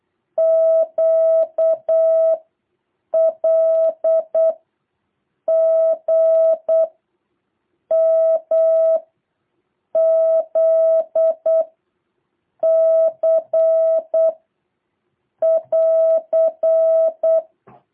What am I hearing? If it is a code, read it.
Morse code, heard as a keyed tone, about 8 wpm, 650 Hz. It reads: Q L G M Z C <AR>
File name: jh8cbh / 2023cw